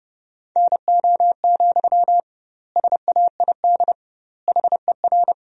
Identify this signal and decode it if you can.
Morse code, heard as a keyed tone, about 30 wpm, 700 Hz. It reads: NO, SAID HER
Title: 2x Word-Spacing - Sentences from Top 300 Words - 30wpm